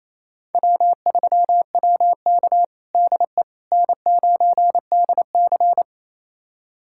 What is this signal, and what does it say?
Morse code, heard as a keyed tone, about 28 wpm, 700 Hz. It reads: W3WK DE N9DC
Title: USA Callsign DE Callsign - 4 Characters - 28wpm